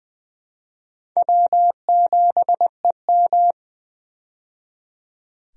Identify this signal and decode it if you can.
Morse code, heard as a keyed tone, about 20 wpm, 700 Hz. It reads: W7EM
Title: USA Call Signs - 4 Characters - 20wpm